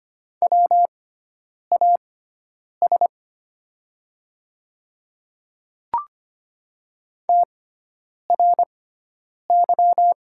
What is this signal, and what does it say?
Morse code, heard as a keyed tone, about 25 wpm, spaced as for 9 wpm, 700 Hz. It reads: WAS TRY
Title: 6x Character Spacing - Top 200 Words - 25wpm